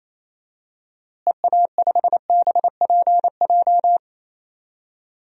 Morse code, heard as a keyed tone, about 28 wpm, 700 Hz. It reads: EA5BPJ
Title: DXCC-100 Call Signs - 6 Characters - Repeated 3x - 28wpm